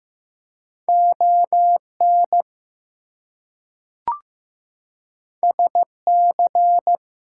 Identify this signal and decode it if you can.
Morse code, heard as a keyed tone, about 15 wpm, 700 Hz. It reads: ON SC
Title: US States and Canadian Provinces 15wpm